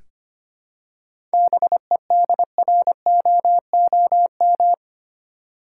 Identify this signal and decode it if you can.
Morse code, heard as a keyed tone, about 25 wpm, 700 Hz. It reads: BEDROOM
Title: Seven Letter Words - 25wpm